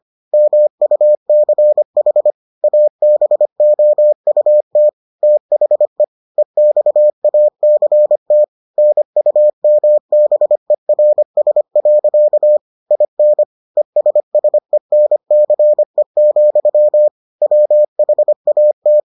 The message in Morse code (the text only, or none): MUCH ABOUT THE EXACT NUMBERS. IN ESSENCE, WHAT